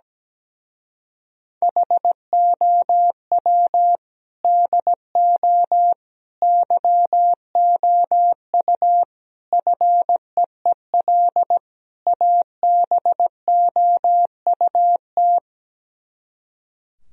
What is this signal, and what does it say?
Morse code, heard as a keyed tone, about 17 wpm, 700 Hz. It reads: HOW DO YOU FEEL ABOUT
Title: Speed-Racing - Sets of 5 Words 17wpm